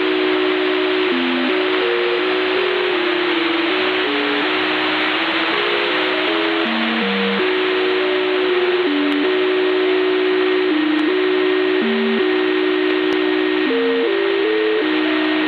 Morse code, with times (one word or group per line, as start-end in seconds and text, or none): none